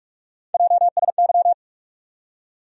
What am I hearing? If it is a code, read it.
Morse code, heard as a keyed tone, about 46 wpm, 700 Hz. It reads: JSY
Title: USA Call Sign Postfixes - 3 Letters 45wpm